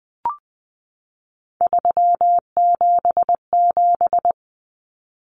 Morse code, heard as a keyed tone, about 20 wpm, 700 Hz. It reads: 377
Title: Numbers - 3-Digit 20wpm